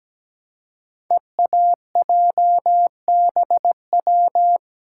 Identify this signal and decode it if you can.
Morse code, heard as a keyed tone, about 17 wpm, 700 Hz. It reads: EAJBW
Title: Code Groups - 5 Letters - 17wpm